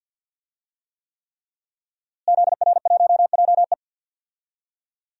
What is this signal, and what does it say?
Morse code, heard as a keyed tone, about 52 wpm, 700 Hz. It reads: ZR1JE
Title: DXCC-100 Call Signs - 5 Characters - Repeated 3x - 50wpm